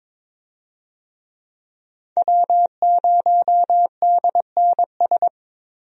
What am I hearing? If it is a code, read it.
Morse code, heard as a keyed tone, about 22 wpm, 700 Hz. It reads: W0DNS
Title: USA Call Signs - 5 Characters - 22wpm